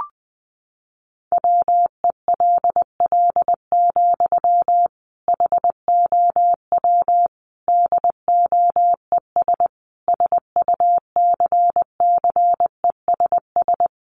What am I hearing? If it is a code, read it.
Morse code, heard as a keyed tone, about 20 wpm, 700 Hz. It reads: WELL, HOW DOES SUCCESS
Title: Sentences from Top 1100 Words 20wpm